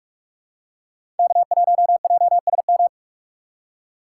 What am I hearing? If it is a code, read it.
Morse code, heard as a keyed tone, about 43 wpm, 700 Hz. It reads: K1JSM
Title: USA Call Signs - 5 Characters - 4x Character Spacing - 45wpm